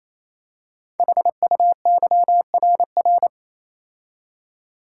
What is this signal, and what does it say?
Morse code, heard as a keyed tone, about 27 wpm, 700 Hz. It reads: HUYRR
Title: Code Groups - 5 Letters - 28wpm